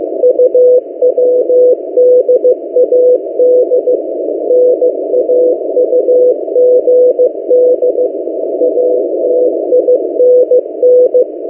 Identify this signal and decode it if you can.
Morse code, heard as a keyed tone, about 15 wpm, 500 Hz. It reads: UWDAD NAUGD ATINN
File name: M08a- -1355utc-8096khz